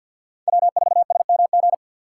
Morse code, heard as a keyed tone, about 50 wpm, 700 Hz. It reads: W4SMG
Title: USA Call Signs - 5 Characters - 2x Character Spacing - 50wpm